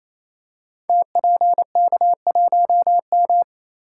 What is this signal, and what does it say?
Morse code, heard as a keyed tone, about 27 wpm, 700 Hz. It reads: TPK1M
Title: Code Groups - 5 Letter:Numbers - 28wpm